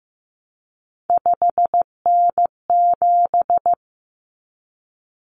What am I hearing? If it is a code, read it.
Morse code, heard as a keyed tone, about 15 wpm, 700 Hz. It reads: 5N7
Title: Common Cut Numbers - 3-Digit 15wpm